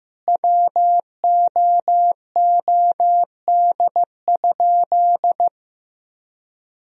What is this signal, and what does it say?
Morse code, heard as a keyed tone, about 16 wpm, 700 Hz. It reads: WOOD?
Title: Sentences from Top 1000 Words - Review Part 2 - 15wpm